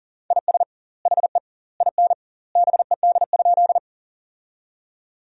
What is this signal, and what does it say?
Morse code, heard as a keyed tone, about 40 wpm, 700 Hz. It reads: IS HE IN BED?
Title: 2x Word-Spacing - Sentences from Top 500 Words - 40wpm